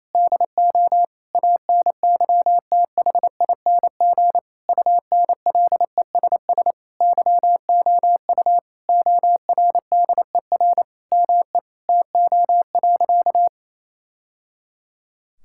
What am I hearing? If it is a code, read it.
Morse code, heard as a keyed tone, about 28 wpm, 700 Hz. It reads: DO ANYTHING UNLESS YOU ORDER ME TO.
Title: Sentences from Top 1000 Words - Review Part 6 - 28wpm